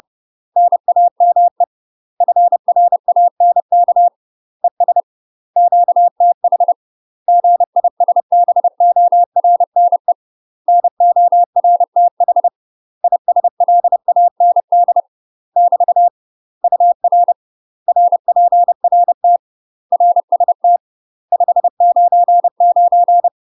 Morse code, heard as a keyed tone, about 30 wpm, 700 Hz. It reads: NAME FRANK ES QTH GISBORNE NORTH ISLAND <BT> UR RPRT RST 599
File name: UK MASTER QSOs TRAINING / 30WPM / UK MASTER QSOs 0019